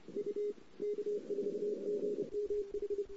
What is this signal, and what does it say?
Morse code, heard as a keyed tone, about 26 wpm, 410 Hz. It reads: U K4GMH
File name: CW / 2008-ss-cw-k5tr-op-n5rz -08:00